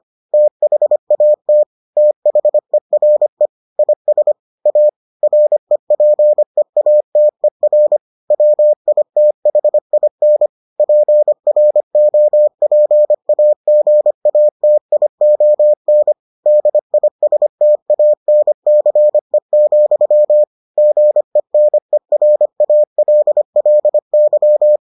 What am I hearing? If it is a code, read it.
Morse code, heard as a keyed tone, about 25 wpm, 600 Hz. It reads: THAT THERE IS A REPEATER WITHIN PROPAGATION DISTANCE, GENERALLY